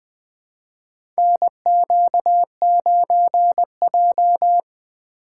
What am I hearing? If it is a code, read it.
Morse code, heard as a keyed tone, about 20 wpm, 700 Hz. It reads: NQ9J
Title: USA Call Signs - 4 Characters - 2x Character Spacing - 20wpm